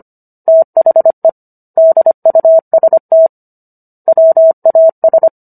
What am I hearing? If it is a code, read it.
Morse code, heard as a keyed tone, about 25 wpm, 650 Hz. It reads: THE DUST WAS